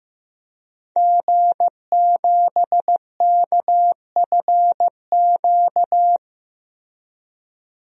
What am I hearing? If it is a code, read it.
Morse code, heard as a keyed tone, about 16 wpm, 700 Hz. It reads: G7KFQ
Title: DXCC-100 Call Signs - 5 Characters - Repeated 2x - 15wpm